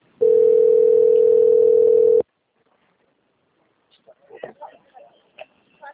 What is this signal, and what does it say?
Ringback tone (North America), 440 + 480 Hz, with speech.
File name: out- - - - - .58949